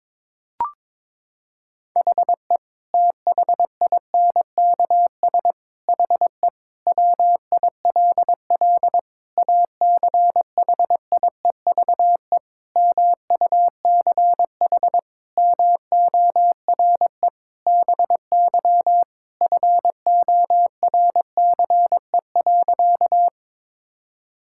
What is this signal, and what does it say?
Morse code, heard as a keyed tone, about 22 wpm, 700 Hz. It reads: HE THINKS HE WILL ACHIEVE MUCH MORE BY FORCE.
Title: Sentences from Top 1000 Words - Review Part 6 - 22wpm